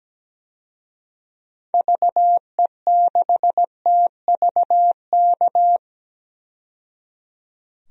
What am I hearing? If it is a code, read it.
Morse code, heard as a keyed tone, about 17 wpm, 700 Hz. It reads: VE6TVK
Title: DXCC-100 Call Signs - 6 Characters - 17wpm